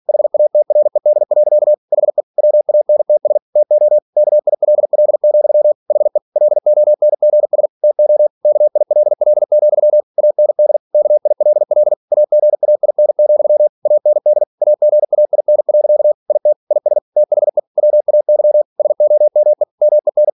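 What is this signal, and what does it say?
Morse code, heard as a keyed tone, about 46 wpm, 600 Hz. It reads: HATRED. HE WANTS TO KILL, HE LONGS TO KILL, AND KILL AGAIN, AND AGAIN. IT IS THE WAY SOME MEN
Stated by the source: Practice text from an English novel sent at 47 WPM.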